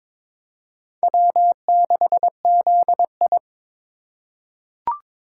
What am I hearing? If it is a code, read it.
Morse code, heard as a keyed tone, about 22 wpm, 700 Hz. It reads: W6ZI
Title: USA Call Signs - 4 Characters - 3x Character Spacing - 22wpm